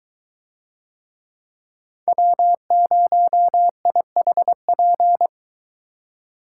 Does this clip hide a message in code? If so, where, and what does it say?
Yes: Morse code from 2.1 to 5.3 s, heard as a keyed tone, about 22 wpm, 700 Hz. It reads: W0IHP